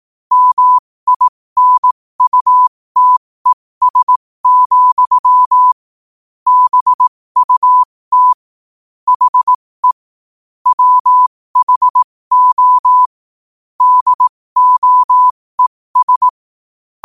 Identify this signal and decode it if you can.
Morse code, heard as a keyed tone, about 18 wpm, spaced as for 15 wpm, 1000 Hz. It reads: MINUTES, BUT HE WHO DOES
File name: 15wpm